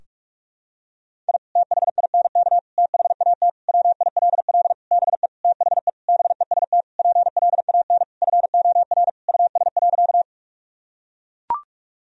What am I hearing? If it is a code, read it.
Morse code, heard as a keyed tone, about 41 wpm, 700 Hz. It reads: I THINK THAT WILL BE THE BEST PLAN FOR US.
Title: Sentences from Top 1000 Words - Review Part 2 - 45wpm